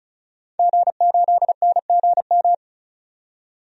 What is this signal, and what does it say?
Morse code, heard as a keyed tone, about 35 wpm, 700 Hz. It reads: G8NGM
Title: DXCC-100 Call Signs - 5 Characters - 35wpm